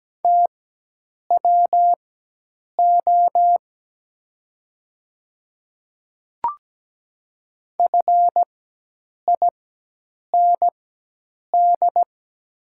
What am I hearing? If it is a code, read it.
Morse code, heard as a keyed tone, about 17 wpm, spaced as for 8 wpm, 700 Hz. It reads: TWO FIND
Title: 4x Character Spacing - Top 200 Words - 17wpm